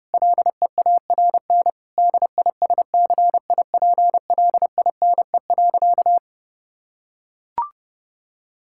Morse code, heard as a keyed tone, about 31 wpm, 700 Hz. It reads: LEARN DISCIPLINE.
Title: Sentences from Top 1000 Words - Review Part 6 - 30wpm